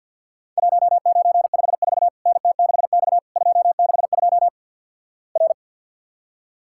Morse code, heard as a keyed tone, about 50 wpm, 700 Hz. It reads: 1954 NT6X 262 R
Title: Worked All Europe DX Contest - QTC Exchange - 50wpm